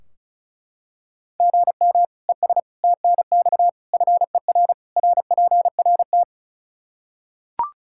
Morse code, heard as a keyed tone, about 36 wpm, 700 Hz. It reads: GM ES TNX FER RPRT
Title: K1USN SST QSO Practice 35wpm